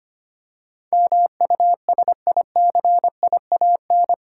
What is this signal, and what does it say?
Morse code, heard as a keyed tone, about 25 wpm, 700 Hz. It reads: MUSICIAN
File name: Eight Letter Words - 25wpm